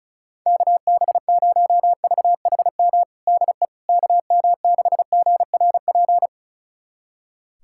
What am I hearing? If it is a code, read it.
Morse code, heard as a keyed tone, about 36 wpm, 700 Hz. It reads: KB0VHM DE KM6GRP
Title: USA Callsign DE Callsign - 6 Characters - 35wpm